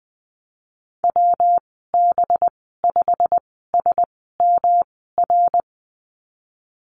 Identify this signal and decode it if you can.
Morse code, heard as a keyed tone, about 20 wpm, spaced as for 15 wpm, 700 Hz. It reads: WB5SMR
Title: 186 Lesson 30g - Letter b - Call Signs 20wpm